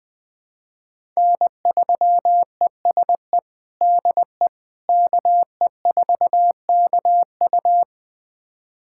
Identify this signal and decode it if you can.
Morse code, heard as a keyed tone, about 20 wpm, 700 Hz. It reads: N3ESE DE KE4KU
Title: USA Callsign DE Callsign - 5 Characters - 20wpm